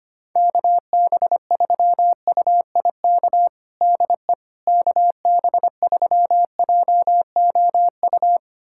Morse code, heard as a keyed tone, about 25 wpm, 700 Hz. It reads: KB3UIK DE KB3JOU
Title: USA Callsign DE Callsign - 6 Characters - 25wpm